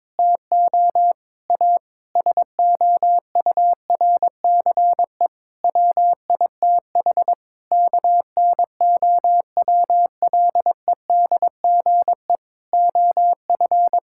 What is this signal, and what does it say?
Morse code, heard as a keyed tone, about 21 wpm, 700 Hz. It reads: TO A SOURCE WITH KNOWLEDGE OF
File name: Sentences with 9-Word Phrases - 22wpm